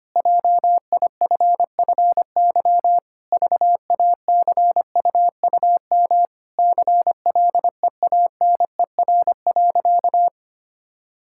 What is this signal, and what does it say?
Morse code, heard as a keyed tone, about 25 wpm, 700 Hz. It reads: JIFFY VACUUM CLEANER.
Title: Pangram Challenge v2 - 25wpm